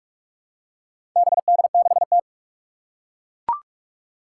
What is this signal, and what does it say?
Morse code, heard as a keyed tone, about 41 wpm, 700 Hz. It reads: BD6T
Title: Four Challenging Letter-Number Combinations 45wpm